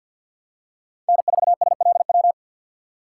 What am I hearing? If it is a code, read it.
Morse code, heard as a keyed tone, about 47 wpm, 700 Hz. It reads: N4SLW
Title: USA Call Signs - 5 Characters - 3x Character Spacing - 50wpm